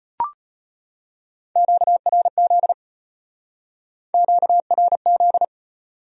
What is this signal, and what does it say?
Morse code, heard as a keyed tone, about 33 wpm, 700 Hz. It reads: QRZ QRZ